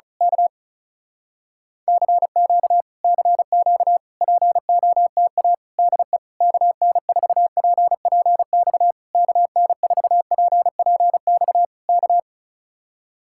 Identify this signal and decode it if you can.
Morse code, heard as a keyed tone, about 33 wpm, 700 Hz. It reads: K CQ CQ POTA DE KN4PPX KN4PPX K
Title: Speed-Racing Common QSO Phrases - 35wpm